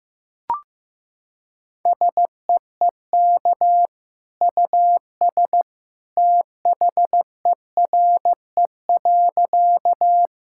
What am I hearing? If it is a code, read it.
Morse code, heard as a keyed tone, about 15 wpm, 700 Hz. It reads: SEEK US THERE.
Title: Sentences from Top 1600 Words 15wpm